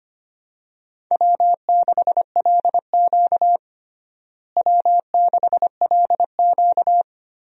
Morse code, heard as a keyed tone, about 25 wpm, 700 Hz. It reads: W6LQ W6LQ
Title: USA Call Signs - 4 Characters - Repeated 3x - 25wpm